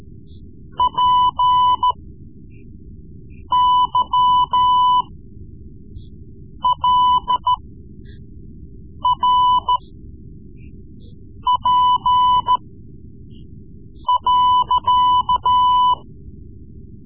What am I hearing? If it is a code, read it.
Morse code, heard as a keyed tone, about 13 wpm, spaced as for 5 wpm, 1000 Hz. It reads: PYLRP.
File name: Morse Lessons / Morse Lessons / Tape-3B